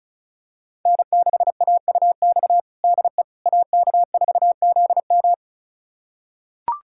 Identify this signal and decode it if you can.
Morse code, heard as a keyed tone, about 36 wpm, 700 Hz. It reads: N6AUX DE AK4ZM
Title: USA Callsign DE Callsign - 5 Characters - 35wpm